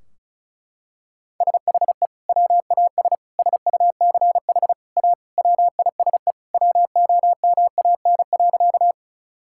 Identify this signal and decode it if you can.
Morse code, heard as a keyed tone, about 34 wpm, 700 Hz. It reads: SHE WAS SUCH A WISE WOMAN.
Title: Sentences from Top 1600 Words 35wpm